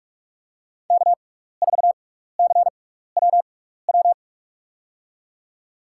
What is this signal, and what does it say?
Morse code, heard as a keyed tone, about 43 wpm, spaced as for 15 wpm, 700 Hz. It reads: K4CWW
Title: USA Call Signs - 5 Characters - 6x Character Spacing - 45wpm